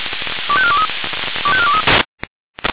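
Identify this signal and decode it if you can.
Five-tone selective call (ZVEI), heard as quick runs of five tones, tones of about 70 ms. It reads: 25423 25423
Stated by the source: ZVEI five-tone selective call received on an SDR.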